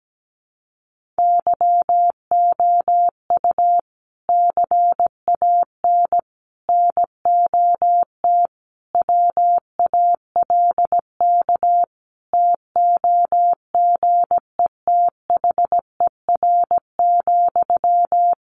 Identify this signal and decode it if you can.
Morse code, heard as a keyed tone, about 17 wpm, 700 Hz. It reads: YOU CAN NOT WALK TOGETHER,
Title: Sentences from Top 1500 Words 17wpm